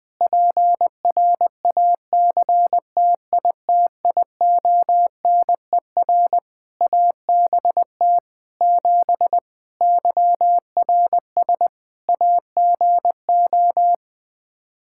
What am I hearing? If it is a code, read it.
Morse code, heard as a keyed tone, about 21 wpm, 700 Hz. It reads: PRACTITIONER ABT 7 YRS AGO
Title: Practice QSO - Exchange 3 - More Info - 20wpm